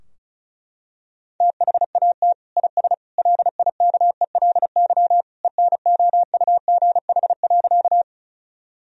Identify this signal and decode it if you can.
Morse code, heard as a keyed tone, about 36 wpm, 700 Hz. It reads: THAT IS LIKELY ENOUGH.
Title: Sentences from Top 1000 Words - Review Part 4 - 35wpm